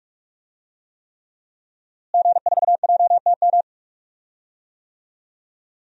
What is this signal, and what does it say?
Morse code, heard as a keyed tone, about 48 wpm, 700 Hz. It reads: G4JTM
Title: DXCC-100 Call Signs - 5 Characters - Repeated 3x - 45wpm